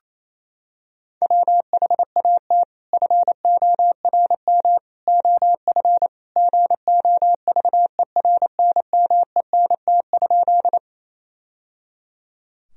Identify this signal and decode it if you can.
Morse code, heard as a keyed tone, about 28 wpm, 700 Hz. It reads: WHAT FORM OF GOVERNMENT?
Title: Sentences from Top 1000 Words - Review Part 3 - 28wpm